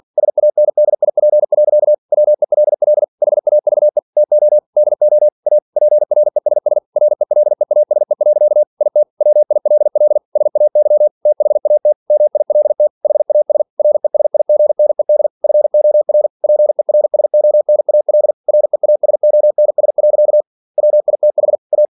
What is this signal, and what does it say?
Morse code, heard as a keyed tone, about 46 wpm, 600 Hz. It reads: SANDIP. WELL HAVE TO DO A PRESS RELEASE. IT WILL SAY THAT MILT HAS RESIGNED FOR PERSONAL REASONS. WITH A
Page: Never 47wpm